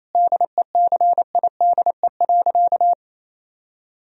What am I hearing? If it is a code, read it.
Morse code, heard as a keyed tone, about 28 wpm, 700 Hz. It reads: DECIDE.
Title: Sentences from Top 1000 Words - Review Part 3 - 28wpm